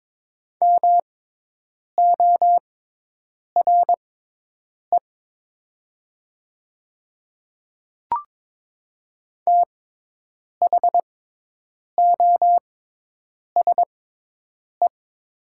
Morse code, heard as a keyed tone, about 22 wpm, spaced as for 8 wpm, 700 Hz. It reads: MORE THOSE
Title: 6x Character Spacing - Top 300 Words - 22wpm